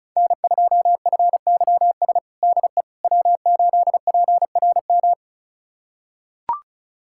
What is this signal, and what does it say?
Morse code, heard as a keyed tone, about 36 wpm, 700 Hz. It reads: N2FYS DE W8PRM
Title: USA Callsign DE Callsign - 5 Characters - 35wpm